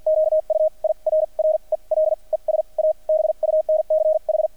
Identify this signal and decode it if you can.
Morse code, heard as a keyed tone, about 50 wpm, spaced as for 39 wpm, 630 Hz. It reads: QUIRREL ES A BUNCH
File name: archived / CRSnet